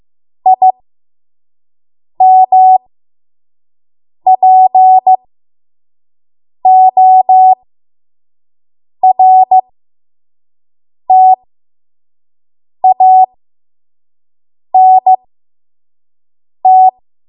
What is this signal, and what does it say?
Morse code, heard as a keyed tone, about 15 wpm, spaced as for 5 wpm, 750 Hz. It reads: IMPORTANT
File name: Morse / 05WPM